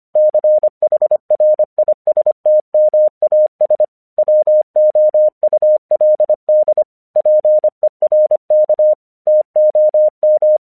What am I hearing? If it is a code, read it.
Morse code, heard as a keyed tone, about 25 wpm, 610 Hz. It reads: CHRISTMAS WOULD PERK TOM
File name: 1 English 25wpm / AQC15